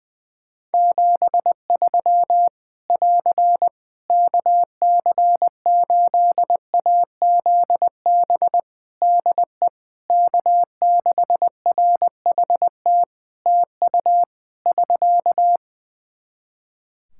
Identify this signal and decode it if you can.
Morse code, heard as a keyed tone, about 20 wpm, 700 Hz. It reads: 73 <AR> KC8AZB DE K6RHT TU <SK>
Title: Practice QSO - Exchange 4 - Ending - 20wpm